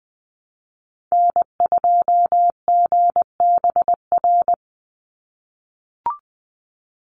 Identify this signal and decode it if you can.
Morse code, heard as a keyed tone, about 20 wpm, 700 Hz. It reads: N2GBR
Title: NA SOTA Activator Callsigns 20wpm